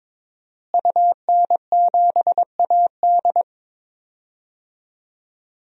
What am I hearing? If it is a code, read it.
Morse code, heard as a keyed tone, about 21 wpm, 700 Hz. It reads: UN7AD
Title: DXCC-100 Call Signs - 5 Characters - Repeated 3x - 22wpm